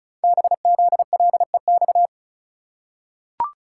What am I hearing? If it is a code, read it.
Morse code, heard as a keyed tone, about 36 wpm, 690 Hz. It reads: BZLEX
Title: Code Groups - 5 Letters - 35wpm